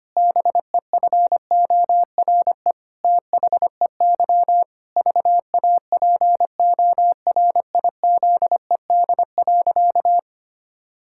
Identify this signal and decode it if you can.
Morse code, heard as a keyed tone, about 25 wpm, 700 Hz. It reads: BEFORE THEY VAPORIZED.
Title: Pangram Challenge v2 - 25wpm